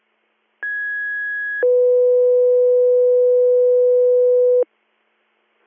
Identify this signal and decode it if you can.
Two-tone page, heard as a one-second tone, then a three-second tone, about 1650 then 500 Hz. It is heard from 0.6 to 4.6 s.